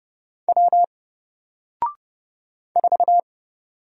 Morse code, heard as a keyed tone, about 29 wpm, 700 Hz. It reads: W 4
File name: Single-Letter-Number-30wpm